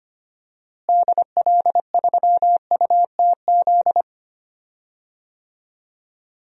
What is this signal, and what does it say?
Morse code, heard as a keyed tone, about 25 wpm, 700 Hz. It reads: DL3UTZ
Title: DXCC-100 Call Signs - 6 Characters - Repeated 3x - 25wpm